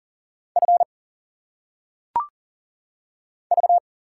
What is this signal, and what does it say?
Morse code, heard as a keyed tone, about 40 wpm, 700 Hz. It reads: F V
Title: Single Letters - Phonetic Pronunciation 40wpm